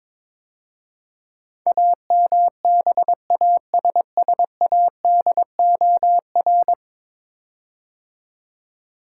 Morse code, heard as a keyed tone, about 22 wpm, 700 Hz. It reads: AMBASSADOR
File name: practice / 22wpm / 10 Letter Words - First 3 Letters Given - 22wpm